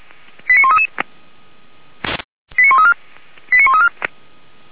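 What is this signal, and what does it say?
Five-tone selective call (ZVEI), heard as quick runs of five tones, tones of about 70 ms. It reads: 89133 89134 89134
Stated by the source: ZVEI five-tone call received by SDR, fire brigade Orlamünde (feuerwehr-orlamuende).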